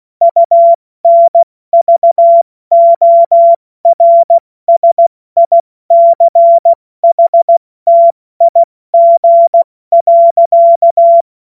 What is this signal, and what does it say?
Morse code, heard as a keyed tone, about 16 wpm, 680 Hz. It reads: UNVORSICHTIG.
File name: Morsekurs / Witze / Witz 17. Mai 2022 70 F80